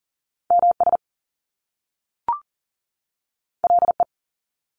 Morse code, heard as a keyed tone, about 39 wpm, 700 Hz. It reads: MS LE
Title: Top Two Letter Combinations in Words 40wpm